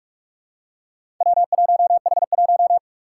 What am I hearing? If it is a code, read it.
Morse code, heard as a keyed tone, about 46 wpm, 700 Hz. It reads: W1H1
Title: Four Challenging Letter-Number Combinations 45wpm